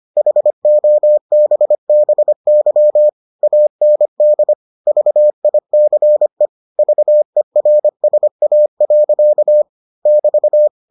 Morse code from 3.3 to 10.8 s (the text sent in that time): AND VICE VERSA. <BT>